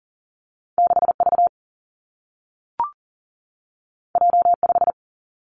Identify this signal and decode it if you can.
Morse code, heard as a keyed tone, about 41 wpm, 700 Hz. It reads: 6V J5
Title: Two Challenging Letter-Number Combinations 40wpm